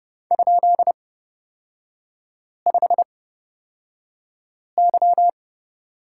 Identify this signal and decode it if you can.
Morse code, heard as a keyed tone, about 30 wpm, spaced as for 5 wpm, 700 Hz. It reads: ?5Y
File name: Lesson 38e - Letter q - Three Characters 30wpm